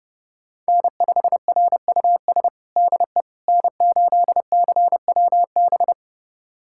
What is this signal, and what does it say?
Morse code, heard as a keyed tone, about 32 wpm, 700 Hz. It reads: N5RUS DE N8CWB
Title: USA Callsign DE Callsign - 5 Characters - 30wpm